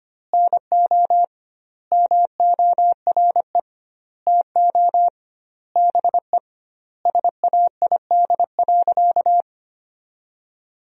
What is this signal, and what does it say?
Morse code, heard as a keyed tone, about 25 wpm, 700 Hz. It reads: NO MORE TO BE SAID.
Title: 2x Word-Spacing - Sentences from Top 100 Words - 25wpm